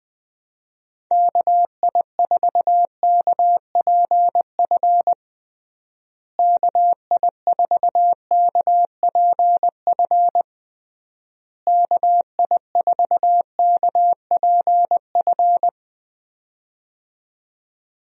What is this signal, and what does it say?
Morse code, heard as a keyed tone, about 20 wpm, 700 Hz. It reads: KI4KPF KI4KPF KI4KPF
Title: USA Call Signs - 6 Characters - Repeated 3x - 20wpm